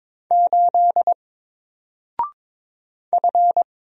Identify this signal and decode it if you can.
Morse code, heard as a keyed tone, about 22 wpm, 700 Hz. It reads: 8 F
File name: Single-Letter-Number-22wpm